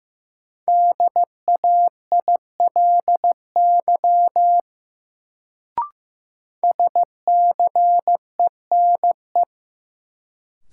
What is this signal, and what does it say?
Morse code, heard as a keyed tone, about 15 wpm, 700 Hz. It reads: DAILY SCENE
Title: Sentences from Top 1100 Words - New Word Drill - 15wpm